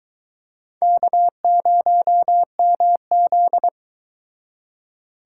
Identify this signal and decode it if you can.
Morse code, heard as a keyed tone, about 23 wpm, 700 Hz. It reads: K0MZ